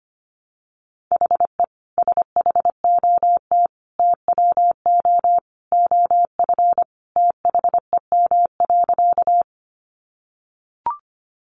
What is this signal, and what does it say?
Morse code, heard as a keyed tone, about 25 wpm, 700 Hz. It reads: HE SHOT TWO OF THEM.
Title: Sentences from Top 1100 Words 25wpm